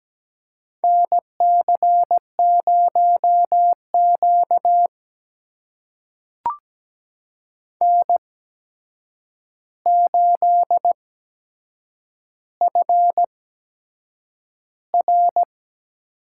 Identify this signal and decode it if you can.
Morse code, heard as a keyed tone, about 17 wpm, spaced as for 7 wpm, 700 Hz. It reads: NC0Q N8FR
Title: USA Call Signs - 4 Characters - 8x Character Spacing - 17wpm